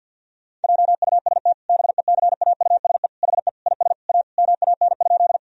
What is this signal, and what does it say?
Morse code, heard as a keyed tone, about 52 wpm, 700 Hz. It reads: JUST BECAUSE HE IS A MAN?
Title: Sentences from Top 1000 Words - Review Part 3 - 50wpm